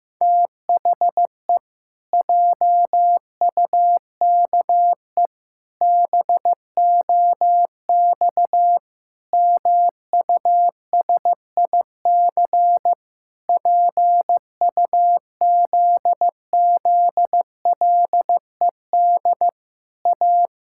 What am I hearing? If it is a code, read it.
Morse code, heard as a keyed tone, about 15 wpm, 700 Hz. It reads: THE JUKE BOX MUSIC PUZZLED A
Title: Pangram Challenge v2 - 15wpm